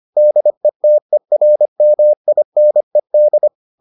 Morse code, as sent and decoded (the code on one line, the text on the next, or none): -.. - . .-. -- .. -. -..
DETERMINED